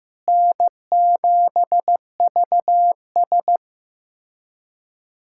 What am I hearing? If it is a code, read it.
Morse code, heard as a keyed tone, about 15 wpm, 700 Hz. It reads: N7VS
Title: K1USN SST QSO Practice - 15wpm